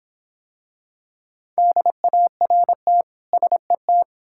Morse code, heard as a keyed tone, about 26 wpm, 700 Hz. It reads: DART SET